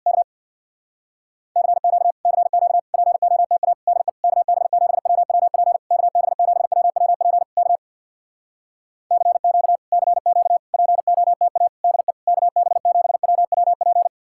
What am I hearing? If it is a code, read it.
Morse code, heard as a keyed tone, about 64 wpm, 700 Hz. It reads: K CQ CQ POTA DE KB7WWP KB7WWP K CQ CQ POTA DE KB7WWP